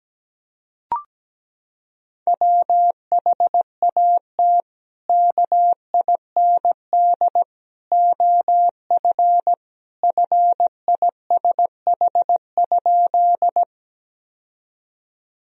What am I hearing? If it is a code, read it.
Morse code, heard as a keyed tone, about 17 wpm, 700 Hz. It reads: WHAT KIND OF FISH?
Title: Sentences from Top 1000 Words - Review Part 1 - 17wpm